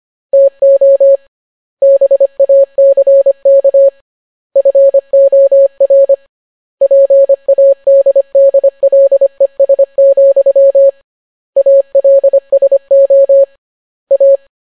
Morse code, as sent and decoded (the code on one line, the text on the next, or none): - --- / -... .- -.-. -.- / ..-. --- .-. / .--. .- -.. -.. .-.. ... --..-- / .- .-.. ... --- / .-
TO BACK FOR PADDLES, ALSO A